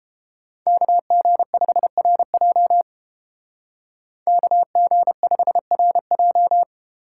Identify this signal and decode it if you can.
Morse code, heard as a keyed tone, about 30 wpm, 700 Hz. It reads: KG5RJ KG5RJ